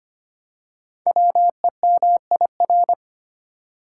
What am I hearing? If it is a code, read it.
Morse code, heard as a keyed tone, about 25 wpm, 700 Hz. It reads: WEMIR
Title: Code Groups - 5 Letters - 25wpm